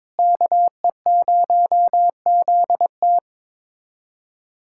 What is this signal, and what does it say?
Morse code, heard as a keyed tone, about 21 wpm, 700 Hz. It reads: KE0ZT
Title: USA Callsign DE Callsign - 5 Characters - 22wpm